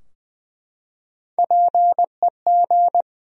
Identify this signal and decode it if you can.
Morse code, heard as a keyed tone, about 20 wpm, 700 Hz. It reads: PEG
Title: USA Common Names - Max 3 Letters - 20wpm